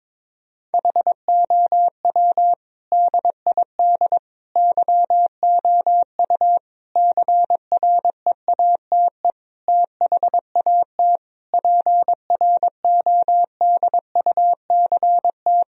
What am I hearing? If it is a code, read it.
Morse code, heard as a keyed tone, about 22 wpm, 700 Hz. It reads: HOW DID YOU CREATE THAT PRODUCT
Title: Sentences from Top 1000 Words - Review Part 3 - 22wpm